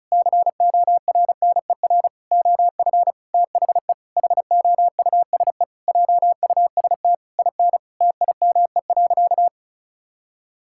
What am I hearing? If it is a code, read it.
Morse code, heard as a keyed tone, about 36 wpm, 700 Hz. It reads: CORNER OF THE HOUSE JUST IN TIME.
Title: Sentences from Top 1600 Words 35wpm